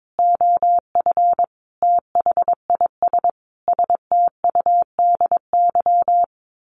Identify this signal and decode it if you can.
Morse code, heard as a keyed tone, about 22 wpm, 700 Hz. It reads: OF THIS STUDY
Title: Sets of 5 Words Encore2 22wpm